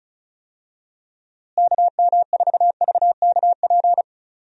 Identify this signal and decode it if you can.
Morse code, heard as a keyed tone, about 34 wpm, 700 Hz. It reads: KM4VKP
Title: USA Call Signs - 6 Characters - Repeated 2x - 35wpm